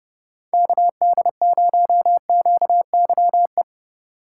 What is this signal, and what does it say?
Morse code, heard as a keyed tone, about 30 wpm, 700 Hz. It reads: KD0QYE
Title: USA Call Signs - 6 Characters - 8x Character Spacing - 30wpm